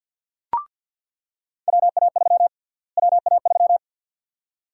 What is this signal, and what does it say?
Morse code, heard as a keyed tone, about 50 wpm, 700 Hz. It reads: WA3 WA3
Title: USA Call Sign Prefixes - Repeated 2x - 50wpm